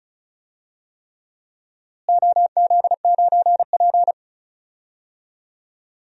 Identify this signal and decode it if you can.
Morse code, heard as a keyed tone, about 34 wpm, 700 Hz. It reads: OZ9P
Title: DXCC-100 Call Signs - 4 Characters - Repeated 3x - 35wpm